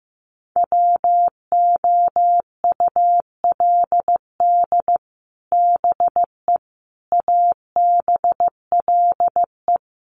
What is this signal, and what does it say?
Morse code, heard as a keyed tone, about 15 wpm, 700 Hz. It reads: WOULD BE ABLE
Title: Sets of 4 Words 15wpm